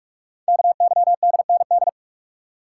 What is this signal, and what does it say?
Morse code, heard as a keyed tone, about 44 wpm, 700 Hz. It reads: KYDND
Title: Code Groups - 5 Letters - 45wpm